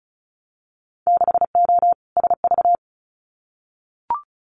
Morse code, heard as a keyed tone, about 36 wpm, 700 Hz. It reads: 6O SV
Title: Winter Field Day Exchanges Encore 35wpm